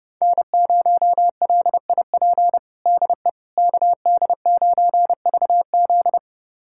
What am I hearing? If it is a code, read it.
Morse code, heard as a keyed tone, about 32 wpm, 700 Hz. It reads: N0LIP DE KD9VZ
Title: USA Callsign DE Callsign - 5 Characters - 30wpm